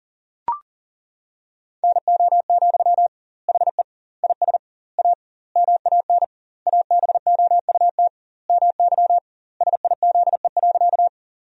Morse code, heard as a keyed tone, about 40 wpm, 700 Hz. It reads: NO, HE IS A MAN ABOUT MY SIZE.
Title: 2x Word-Spacing - Sentences from Top 400 Words - 40wpm